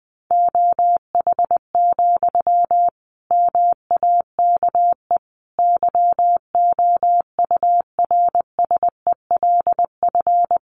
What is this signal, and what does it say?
Morse code, heard as a keyed tone, about 20 wpm, 700 Hz. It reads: OH, MAKE YOURSELF
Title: Sentences from Top 700 Words 20wpm